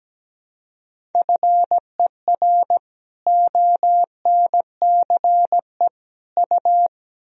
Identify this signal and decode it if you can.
Morse code, heard as a keyed tone, about 17 wpm, 700 Hz. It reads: FER ONCE U
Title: Sentences from Top 300 Words and Abbreviations 17wpm